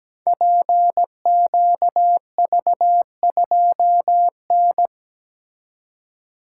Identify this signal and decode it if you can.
Morse code, heard as a keyed tone, about 17 wpm, 700 Hz. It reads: PQV2N
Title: Code Groups - 5 Letter:Numbers - 17wpm